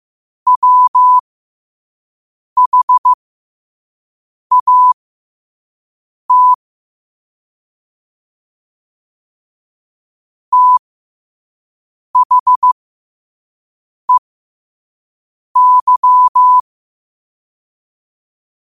Morse code, heard as a keyed tone, about 15 wpm, spaced as for 5 wpm, 1000 Hz. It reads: WHAT THEY
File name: qotdfiles / 5wpm